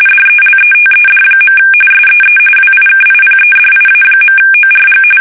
RTTY, about 45.45 baud. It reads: DELETED. SEE WIKIPEDIA.ORG
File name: fsk850